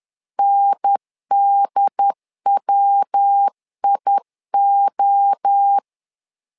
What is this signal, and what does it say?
Morse code, heard as a keyed tone, about 10 wpm, 800 Hz. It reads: NDWIO
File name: LEZIONI / 4 Q 10wpm 2210